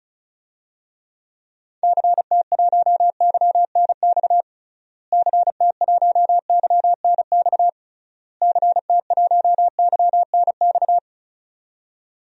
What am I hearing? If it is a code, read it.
Morse code, heard as a keyed tone, about 36 wpm, 700 Hz. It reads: CT1YNX CT1YNX CT1YNX
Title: DXCC-100 Call Signs - 6 Characters - Repeated 3x - 35wpm